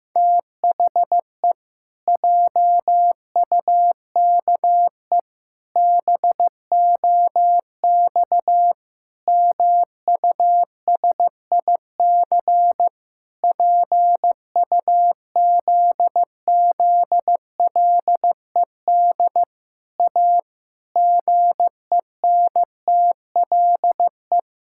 Morse code, heard as a keyed tone, about 15 wpm, 700 Hz. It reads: THE JUKE BOX MUSIC PUZZLED A GENTLE
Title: Pangram Challenge v2 - 15wpm